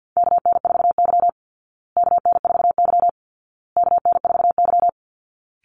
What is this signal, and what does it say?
Morse code, heard as a keyed tone, about 50 wpm, 700 Hz. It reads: KN4Y KN4Y KN4Y
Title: Call Signs Repeated 2x - 50wpm